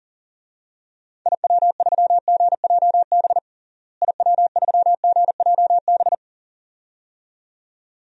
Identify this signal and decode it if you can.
Morse code, heard as a keyed tone, about 40 wpm, 700 Hz. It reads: IW3GJB IW3GJB
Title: DXCC-100 Call Signs - 6 Characters - Repeated 2x - 40wpm